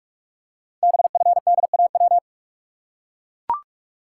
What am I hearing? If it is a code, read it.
Morse code, heard as a keyed tone, about 44 wpm, 690 Hz. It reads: BFDAW
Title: Code Groups - 5 Letters - 45wpm